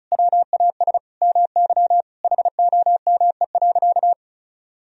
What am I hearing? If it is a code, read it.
Morse code, heard as a keyed tone, about 34 wpm, 700 Hz. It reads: WAS MY HOME.
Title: Sentences from Top 1000 Words - Review Part 1 - 35wpm